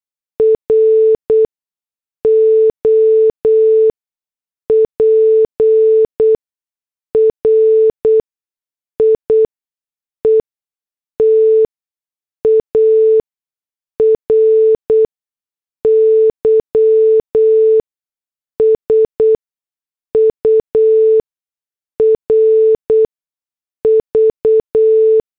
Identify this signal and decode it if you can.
Morse code, heard as a keyed tone, about 8 wpm, spaced as for 6 wpm, 440 Hz. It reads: ROPRIETARY SURV